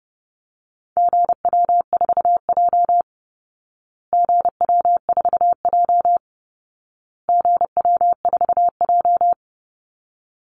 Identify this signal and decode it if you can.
Morse code, heard as a keyed tone, about 30 wpm, 700 Hz. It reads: GW4J GW4J GW4J
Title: Call Signs Repeated 2x - 30wpm